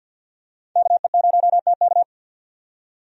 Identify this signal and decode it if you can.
Morse code, heard as a keyed tone, about 50 wpm, 700 Hz. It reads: KE0TK
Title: USA Call Signs - 5 Characters - 50wpm